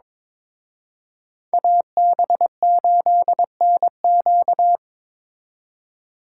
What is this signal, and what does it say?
Morse code, heard as a keyed tone, about 28 wpm, 700 Hz. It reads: AB8NQ
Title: Speed Racing - USA 2x2 Call Signs 22wpm